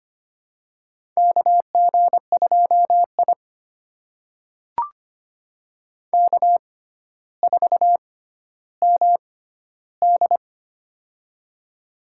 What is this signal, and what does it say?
Morse code, heard as a keyed tone, about 25 wpm, spaced as for 13 wpm, 700 Hz. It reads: KG2I K4MD
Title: USA Call Signs - 4 Characters - 6x Character Spacing - 25wpm